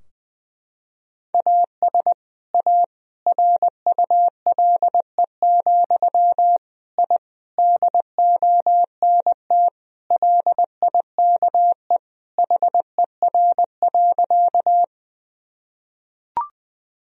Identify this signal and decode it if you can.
Morse code, heard as a keyed tone, about 20 wpm, 700 Hz. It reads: AS A RULE, I DONT LIKE HER.
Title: Sentences from Top 1000 Words - Review Part 2 - 20wpm